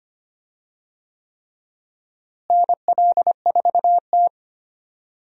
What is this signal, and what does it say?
Morse code, heard as a keyed tone, about 25 wpm, 700 Hz. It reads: NL4T
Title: USA Call Signs - 4 Characters - Repeated 3x - 25wpm